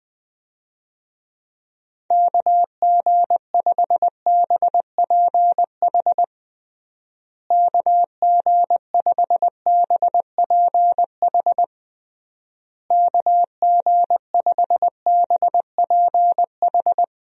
Morse code, heard as a keyed tone, about 21 wpm, 700 Hz. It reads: KG5BPH KG5BPH KG5BPH
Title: USA Call Signs - 6 Characters - Repeated 3x - 20wpm